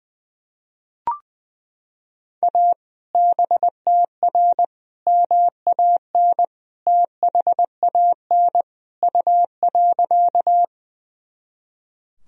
Morse code, heard as a keyed tone, about 20 wpm, 700 Hz. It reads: A BTR MAN THAN U.
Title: Sentences from Top 400 Words and Abbreviations 20wpm